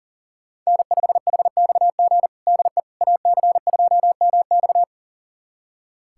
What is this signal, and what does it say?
Morse code, heard as a keyed tone, about 40 wpm, 690 Hz. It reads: N5HXG DE AC2MX
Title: USA Callsign DE Callsign - 5 Characters - 40wpm